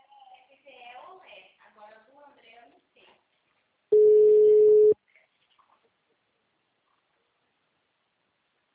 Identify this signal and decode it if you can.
Ringback tone (Europe, 425 Hz), with speech.